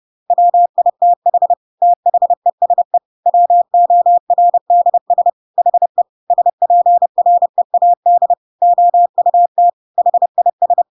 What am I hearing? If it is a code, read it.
Morse code, heard as a keyed tone, about 30 wpm, 700 Hz. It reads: WITH THESE WORDS HE SPREAD OUT HIS
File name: Pinocchio / Pinocchio ch5 30 30 ik1hge